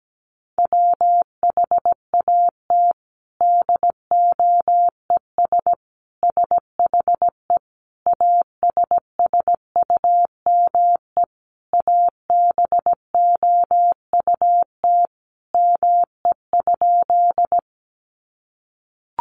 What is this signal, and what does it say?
Morse code, heard as a keyed tone, about 17 wpm, 700 Hz. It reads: WHAT DOES SHE ASSUME ABOUT ME?
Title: Sentences from Top 1500 Words 17wpm